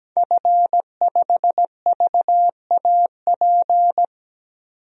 Morse code, heard as a keyed tone, about 17 wpm, 700 Hz. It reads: F5VAP
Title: DXCC-100 Call Signs - 5 Characters - 17wpm